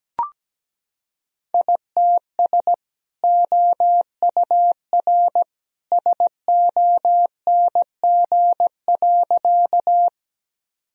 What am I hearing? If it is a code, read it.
Morse code, heard as a keyed tone, about 17 wpm, 700 Hz. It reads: ITS OUR SONG.
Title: Sentences from Top 1000 Words - Review Part 2 - 17wpm